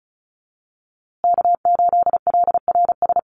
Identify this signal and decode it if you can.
Morse code, heard as a keyed tone, about 34 wpm, 700 Hz. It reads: K8LRS
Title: USA Call Signs Repeated 2x 35wpm